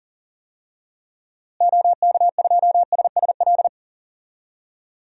Morse code, heard as a keyed tone, about 37 wpm, 700 Hz. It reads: OK2SSL
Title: DXCC-100 Call Signs - 6 Characters - Repeated 2x - 40wpm